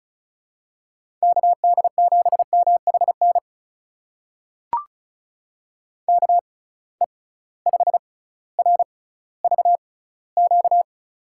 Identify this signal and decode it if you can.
Morse code, heard as a keyed tone, about 36 wpm, spaced as for 18 wpm, 700 Hz. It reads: KD7MHN KE5RVQ
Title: USA Call Signs - 6 Characters - 6x Character Spacing - 35wpm